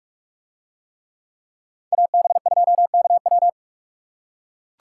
Morse code, heard as a keyed tone, about 44 wpm, 700 Hz. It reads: AB2KW